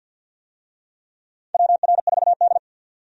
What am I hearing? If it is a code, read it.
Morse code, heard as a keyed tone, about 49 wpm, 700 Hz. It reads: WR4D
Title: USA Call Signs - 4 Characters - 50wpm